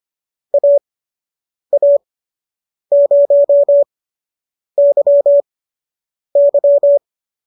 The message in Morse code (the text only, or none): AA0YY